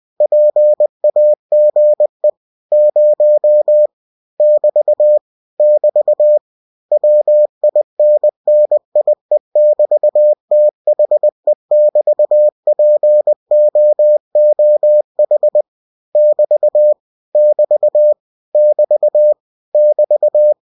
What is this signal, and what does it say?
Morse code, heard as a keyed tone, about 20 wpm, 600 Hz. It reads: PAGE 0 <BT> <BT> WINNIE<BT>THE<BT>POOH <BT> <BT> <BT> <BT>
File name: cw-books / winnie-the-pooh / winnie-20wpm-page-0000